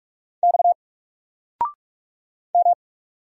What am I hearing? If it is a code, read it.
Morse code, heard as a keyed tone, about 46 wpm, 700 Hz. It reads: X M
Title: Single Letter-Number 45wpm